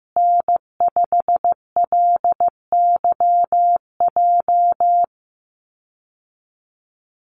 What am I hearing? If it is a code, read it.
Morse code, heard as a keyed tone, about 15 wpm, 700 Hz. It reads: N5LYJ
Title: Call Signs Repeated 3x - 15wpm